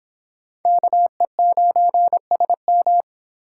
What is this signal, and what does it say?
Morse code, heard as a keyed tone, about 25 wpm, 700 Hz. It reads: KE9SM